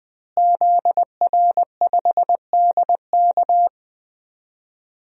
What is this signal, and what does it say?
Morse code, heard as a keyed tone, about 20 wpm, 700 Hz. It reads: ZR5DK
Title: DXCC-100 Call Signs - 5 Characters - Repeated 2x - 20wpm